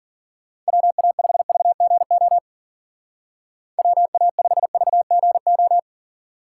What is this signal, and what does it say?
Morse code, heard as a keyed tone, about 50 wpm, 700 Hz. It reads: WA5VGO WA5VGO